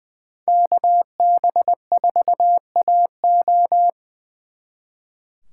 Morse code, heard as a keyed tone, about 24 wpm, 700 Hz. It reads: KB4AO
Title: Speed Racing - USA Call Signs 20wpm